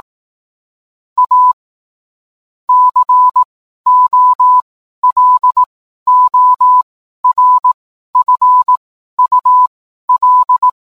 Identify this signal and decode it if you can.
Morse code, heard as a keyed tone, about 18 wpm, spaced as for 12 wpm, 1000 Hz. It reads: A COLORFUL